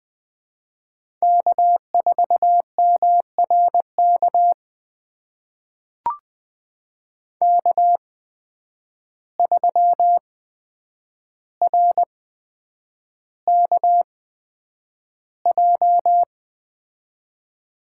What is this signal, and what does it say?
Morse code, heard as a keyed tone, about 20 wpm, spaced as for 9 wpm, 700 Hz. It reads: K4MRK K3RKJ
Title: USA Call Signs - 8x Character Spacing - 20wpm